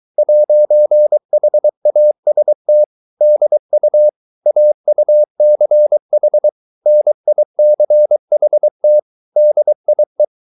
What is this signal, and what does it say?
Morse code, heard as a keyed tone, about 23 wpm, 600 Hz. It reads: 'HAST DU AUCH NICHT DIE